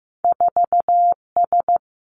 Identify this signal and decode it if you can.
Morse code, heard as a keyed tone, about 15 wpm, 700 Hz. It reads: 4 S
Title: Two Challenging Letter-Number Combinations 15wpm